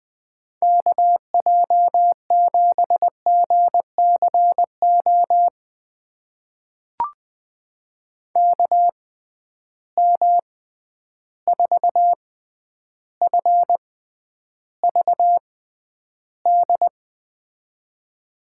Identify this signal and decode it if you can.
Morse code, heard as a keyed tone, about 20 wpm, spaced as for 10 wpm, 700 Hz. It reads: KJ7GCO KM4FVD
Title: USA Call Signs - 6 Characters - 6x Character Spacing - 20wpm